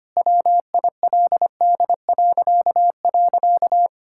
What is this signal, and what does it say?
Morse code, heard as a keyed tone, about 25 wpm, 700 Hz. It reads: WILD..
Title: Pangram Challenge v2 - 25wpm